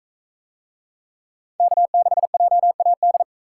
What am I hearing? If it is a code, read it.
Morse code, heard as a keyed tone, about 40 wpm, 700 Hz. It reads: K6JAD